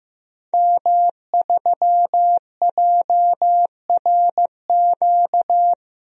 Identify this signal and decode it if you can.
Morse code, heard as a keyed tone, about 15 wpm, 700 Hz. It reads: M3JRQ
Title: DXCC-100 Call Signs - 5 Characters - Repeated 2x - 15wpm